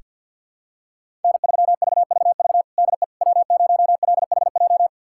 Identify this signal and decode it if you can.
Morse code, heard as a keyed tone, about 47 wpm, 700 Hz. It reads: N3VVV DE W0LHJ
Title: USA Callsign DE Callsign - 5 Characters - 50wpm